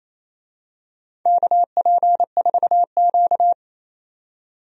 Morse code, heard as a keyed tone, about 28 wpm, 700 Hz. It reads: KP4Q
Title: DXCC-100 Call Signs - 4 Characters - 28wpm